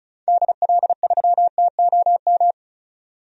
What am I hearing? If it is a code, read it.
Morse code, heard as a keyed tone, about 36 wpm, 700 Hz. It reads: DL3TOM
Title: DXCC-100 Call Signs - 6 Characters - 35wpm